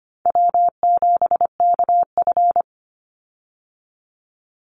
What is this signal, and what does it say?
Morse code, heard as a keyed tone, about 25 wpm, 700 Hz. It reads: W7KF
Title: Call Signs - 25wpm